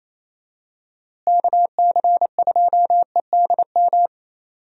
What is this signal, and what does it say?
Morse code, heard as a keyed tone, about 28 wpm, 700 Hz. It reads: KC2EDM
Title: USA Call Signs - 6 Characters - Repeated 2x - 28wpm